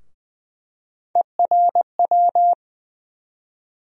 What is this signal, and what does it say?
Morse code, heard as a keyed tone, about 20 wpm, 700 Hz. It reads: ERW
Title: USA Call Sign Postfixes - 3 Letters 20wpm